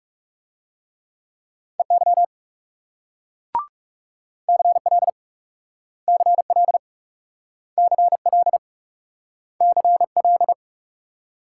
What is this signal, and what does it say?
Morse code, heard as a keyed tone, about 38 wpm, 700 Hz. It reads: EY CL CL CL CL